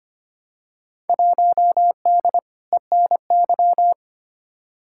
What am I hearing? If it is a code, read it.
Morse code, heard as a keyed tone, about 25 wpm, 700 Hz. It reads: 1D ENY
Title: Field Day Exchanges - 25wpm